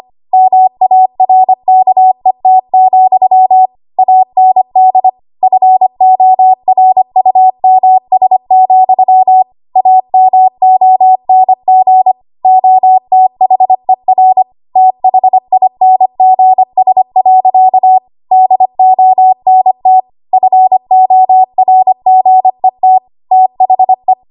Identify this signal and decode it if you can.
Morse code, heard as a keyed tone, about 25 wpm, 750 Hz. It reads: MARKET, AND FORUMS, AMONG OTHER THINGS. DONT FORGET THE
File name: Morse / 25WPM